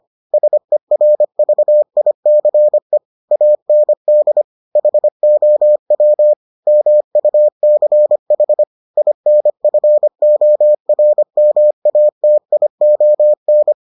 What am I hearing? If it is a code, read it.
Morse code, heard as a keyed tone, about 25 wpm, 600 Hz. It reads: SERVICE AND HOW MUCH INFORMATION